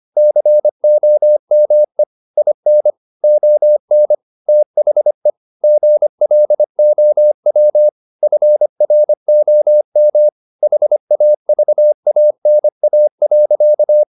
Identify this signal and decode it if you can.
Morse code, heard as a keyed tone, about 25 wpm, 600 Hz. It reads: COME IN ON THE GLOW FROM HAVANA.
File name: the old man and the sea part 1 25wpm